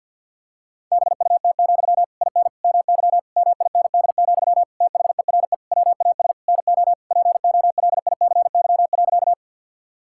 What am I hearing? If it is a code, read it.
Morse code, heard as a keyed tone, about 51 wpm, 690 Hz. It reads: BUT, IN MY MIND, THERE WAS NO POLICY.
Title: Sentences from Top 1000 Words - Review Part 5 - 50wpm